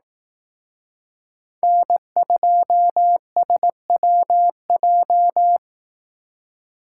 Morse code, heard as a keyed tone, about 18 wpm, 700 Hz. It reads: N2SWJ
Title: Speed Racing - USA 1x3 Call Signs 15wpm